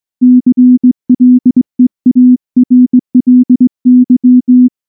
Morse code, heard as a keyed tone, about 19 wpm, 250 Hz. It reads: CLEARLY
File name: morse / morse-training-20WPM-8